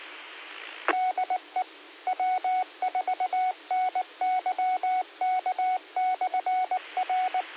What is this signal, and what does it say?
Morse code, heard as a keyed tone, about 19 wpm, 740 Hz. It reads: DE W4NYK/R
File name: qsorec SimplexLogic 13-03-55